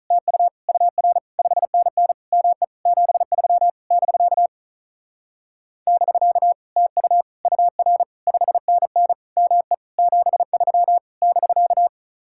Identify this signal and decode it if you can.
Morse code, heard as a keyed tone, about 44 wpm, 700 Hz. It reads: TU UR 5NN ME 73 <BK> <BK> TU UR 5NN ME 73 <BK>